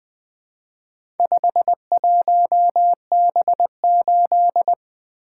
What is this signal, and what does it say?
Morse code, heard as a keyed tone, about 20 wpm, 700 Hz. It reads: 51B8
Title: Four Letter-Number Combinations - 20wpm